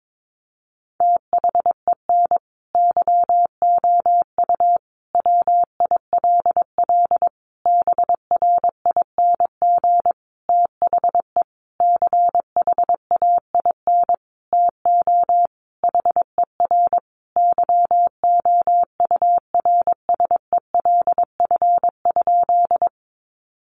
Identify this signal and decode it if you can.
Morse code, heard as a keyed tone, about 22 wpm, 700 Hz. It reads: THEN YOU WILL BRING THE CHAIN TO HER YOURSELF?
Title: Sentences from Top 1500 Words 22wpm